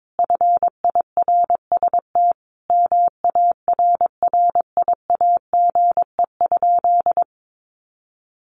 Morse code, heard as a keyed tone, about 22 wpm, 700 Hz. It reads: FIRST MARRIAGE?
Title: Sentences from Top 1500 Words 22wpm